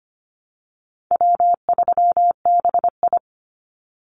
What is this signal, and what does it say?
Morse code, heard as a keyed tone, about 25 wpm, 700 Hz. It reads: W3BI
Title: Easy Call Signs - 25wpm